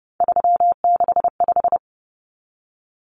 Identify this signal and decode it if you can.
Morse code, heard as a keyed tone, about 30 wpm, 700 Hz. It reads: 365
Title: Common Cut Numbers - 3-Digit 30wpm